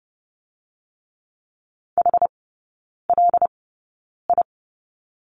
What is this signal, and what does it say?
Morse code, heard as a keyed tone, about 30 wpm, spaced as for 9 wpm, 700 Hz. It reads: HLI
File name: 081 Lesson 14e - Number 2 - Three Characters 30wpm